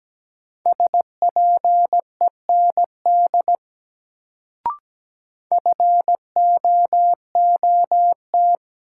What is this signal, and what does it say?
Morse code, heard as a keyed tone, about 17 wpm, 700 Hz. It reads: SPEND FOOT
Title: Sets of 5 Words - New Word Drill - 17wpm